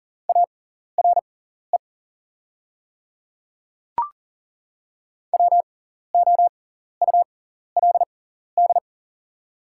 Morse code, heard as a keyed tone, about 40 wpm, spaced as for 14 wpm, 700 Hz. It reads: ARE WOULD
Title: 6x Character Spacing - Top 100 Words - 40wpm